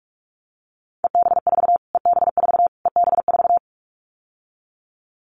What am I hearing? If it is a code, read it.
Morse code, heard as a keyed tone, about 44 wpm, 700 Hz. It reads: EB4 EB4 EB4
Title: Callsign Prefixes for Top 25 DXCC - Repeated 3x - 45wpm